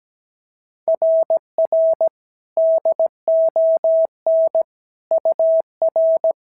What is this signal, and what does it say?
Morse code, heard as a keyed tone, about 16 wpm, 650 Hz. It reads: RR DON UR
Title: SOTA QSO Practice 17wpm